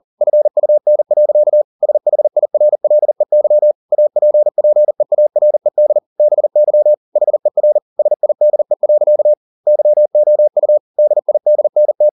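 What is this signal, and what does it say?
Morse code, heard as a keyed tone, about 40 wpm, 600 Hz. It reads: FUN. SHIRLEY APPEARED BY HER SIDE. YOU DIDNT
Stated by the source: Morse code at 40 wpm, practice text, CW med Gnister.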